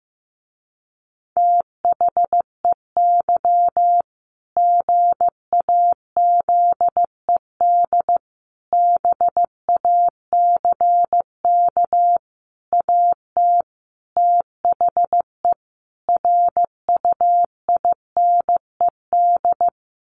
Morse code, heard as a keyed tone, about 15 wpm, 700 Hz. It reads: THEY GAZED BACK AT THE RUINED
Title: A Sound Of Thunder 15wpm